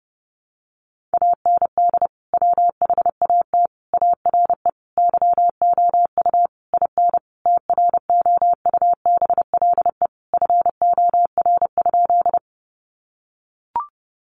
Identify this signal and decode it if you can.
Morse code, heard as a keyed tone, about 30 wpm, 700 Hz. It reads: AND WHAT ARE YOU IN TROUBLE FOR?
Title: Sentences from Top 1200 Words 30wpm